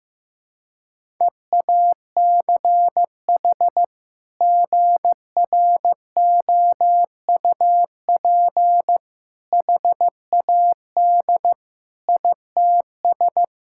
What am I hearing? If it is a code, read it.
Morse code, heard as a keyed tone, about 15 wpm, 700 Hz. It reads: EACH GROUP HAD ITS
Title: Sentences from Top 1000 Words - Review Part 6 - 15wpm